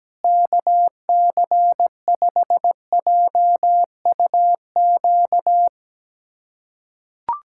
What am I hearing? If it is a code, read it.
Morse code, heard as a keyed tone, about 17 wpm, 700 Hz. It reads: KC5JUQ
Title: USA Callsign DE Callsign - 6 Characters - 17wpm